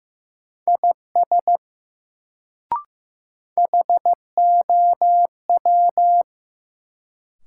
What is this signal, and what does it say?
Morse code, heard as a keyed tone, about 16 wpm, 700 Hz. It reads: IS HOW
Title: Sentences from Top 100 Words - New Word Drill - 15wpm